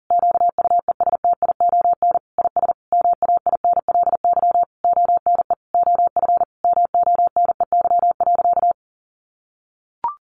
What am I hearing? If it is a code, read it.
Morse code, heard as a keyed tone, about 40 wpm, 700 Hz. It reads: QUESTION IS MAINLY ONE OF MONEY.
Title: Sentences from Top 1400 Words 40wpm